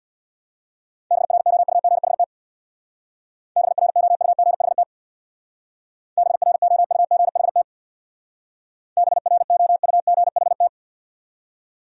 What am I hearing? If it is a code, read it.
Morse code, heard as a keyed tone, about 66 wpm, 700 Hz. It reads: BROUGHT BROUGHT BROUGHT BROUGHT